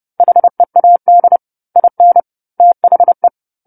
Morse code, heard as a keyed tone, about 30 wpm, 700 Hz. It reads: HEAD IN THE
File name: 10-Chapter-10 30 WPM